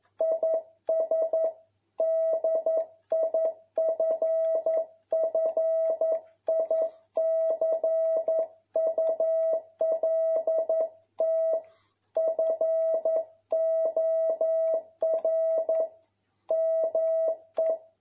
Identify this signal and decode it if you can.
Morse code, heard as a keyed tone, about 11 wpm, 650 Hz. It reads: IS DIFFICULT FOR ME
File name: jh8cbh / 2021cw